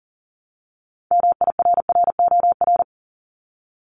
Morse code, heard as a keyed tone, about 39 wpm, 700 Hz. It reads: MIRROR
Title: Multi-letter Phonemes 40wpm